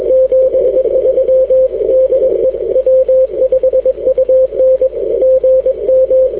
Morse code, heard as a keyed tone, about 21 wpm, 520 Hz. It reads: 73 DE W5UN GM
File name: w5un 160m